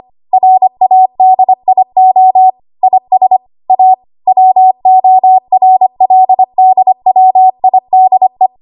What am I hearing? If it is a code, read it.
Morse code, heard as a keyed tone, about 25 wpm, 750 Hz. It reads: RADIO IS A WORLDWIDE
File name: Morse / 25WPM